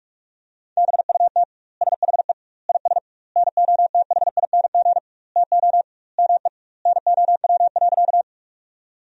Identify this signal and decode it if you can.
Morse code, heard as a keyed tone, about 41 wpm, 700 Hz. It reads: BUT SHE IS NOTHING TO ME NOW.
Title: 2x Word-Spacing - Sentences from Top 500 Words - 45wpm